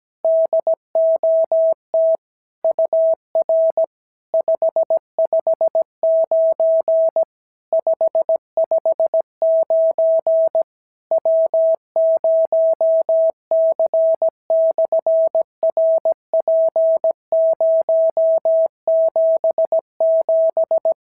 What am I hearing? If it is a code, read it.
Morse code, heard as a keyed tone, about 17 wpm, 650 Hz. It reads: DOT UR 559 559 W0C/RP077
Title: SOTA QSO Practice 17wpm